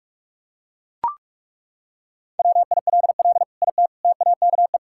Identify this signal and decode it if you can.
Morse code, heard as a keyed tone, about 41 wpm, 700 Hz. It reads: WILL IT TAKE
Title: Sentences from Top 1000 Words - Review Part 6 - 45wpm